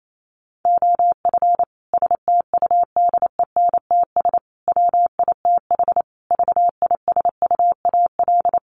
Morse code, heard as a keyed tone, about 28 wpm, 700 Hz. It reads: OF STUDENTS WITH VISUAL
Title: Sets of 5 Words Encore 28wpm